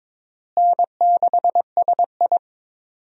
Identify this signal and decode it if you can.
Morse code, heard as a keyed tone, about 22 wpm, 700 Hz. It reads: N6SI
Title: USA Call Signs - 4 Characters - Repeated 3x - 22wpm